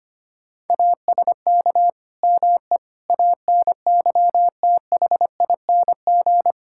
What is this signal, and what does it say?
Morse code, heard as a keyed tone, about 25 wpm, 700 Hz. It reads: ASK ME ANYTHING
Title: Sentences from Top 1600 Words 25wpm